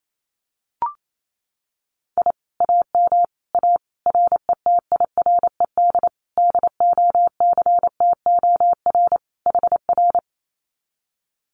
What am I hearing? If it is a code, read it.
Morse code, heard as a keyed tone, about 28 wpm, 700 Hz. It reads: I AM A RETIRED DOCTOR HR
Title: Common QSO Phrases Encore 28wpm